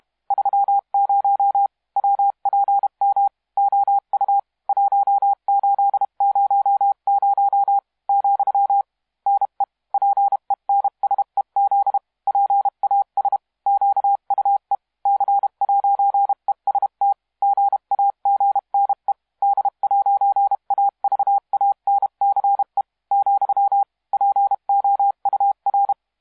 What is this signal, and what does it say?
Morse code, heard as a keyed tone, about 31 wpm, 800 Hz. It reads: 30 WPM OU 1800 , NE PENSEZ PAS QUE C'EST GAGNE D'AVANCE , POUR
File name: musiqueradio / cw hsc demo f6ddr